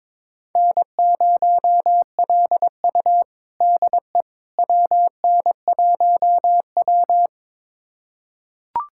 Morse code, heard as a keyed tone, about 22 wpm, 700 Hz. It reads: N0LU DE WN1W
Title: USA Callsign DE Callsign - 4 Characters - 22wpm